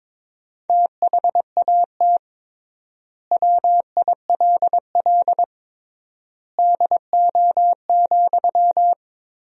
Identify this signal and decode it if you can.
Morse code, heard as a keyed tone, about 22 wpm, 700 Hz. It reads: THAT WILL DO,
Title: 3x Word-Spacing - Sentences from Top 100 Words - 22wpm